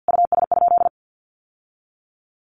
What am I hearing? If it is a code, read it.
Morse code, heard as a keyed tone, about 50 wpm, 700 Hz. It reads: US?
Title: Sentences from Top 1100 Words 50wpm